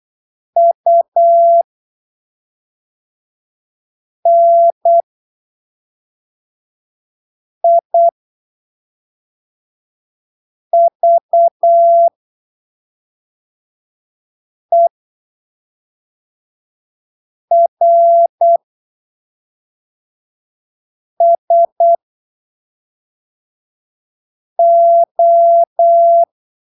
Morse code, heard as a keyed tone, about 8 wpm, spaced as for 3 wpm, 680 Hz. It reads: UNIVERSO
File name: arquivos / texto / texto 17 08ppm